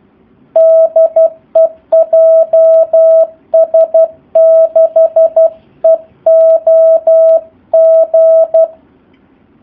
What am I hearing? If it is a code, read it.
Morse code, heard as a keyed tone, about 12 wpm, 650 Hz. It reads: DE JS6EOG
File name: jh8cbh / 2019cw / SP04- w